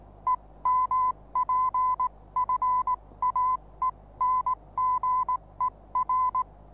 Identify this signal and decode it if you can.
Morse code, heard as a keyed tone, about 19 wpm, spaced as for 16 wpm, 1010 Hz. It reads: EMPFAENGER
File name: repos / Morsekurs / DJ4UF Morsen26